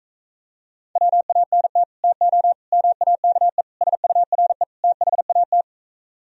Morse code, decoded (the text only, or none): WANT TO MAKE SURE THAT